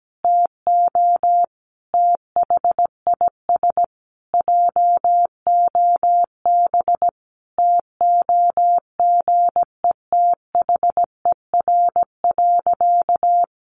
Morse code, heard as a keyed tone, about 17 wpm, 700 Hz. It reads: TO THIS JOB TOGETHER.
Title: Sentences from Top 1300 Words 17wpm